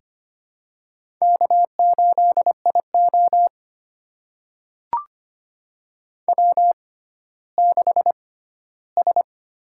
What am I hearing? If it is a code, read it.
Morse code, heard as a keyed tone, about 25 wpm, spaced as for 14 wpm, 700 Hz. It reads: K8IO W6S
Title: USA Call Signs - 4 Characters - 6x Character Spacing - 25wpm